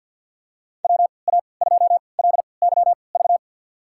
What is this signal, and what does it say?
Morse code, heard as a keyed tone, about 46 wpm, spaced as for 27 wpm, 700 Hz. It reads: WA2LYV
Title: USA Call Signs - 3x Character Spacing - 50wpm